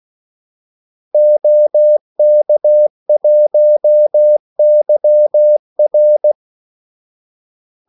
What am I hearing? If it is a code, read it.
Morse code, heard as a keyed tone, about 16 wpm, 600 Hz. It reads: OK1YR
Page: Kaldesignaler 16 wpm